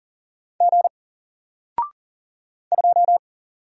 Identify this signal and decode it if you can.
Morse code, heard as a keyed tone, about 40 wpm, 700 Hz. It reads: G 2
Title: Single Character (Letter:Number:Punctuation:Prosign) 40wpm